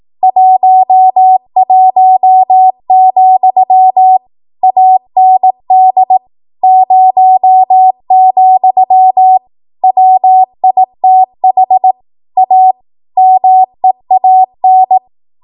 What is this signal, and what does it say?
Morse code, heard as a keyed tone, about 18 wpm, 750 Hz. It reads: 11, AND 0, WITH A MEAN
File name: Archive / 18 WPM / 18WPM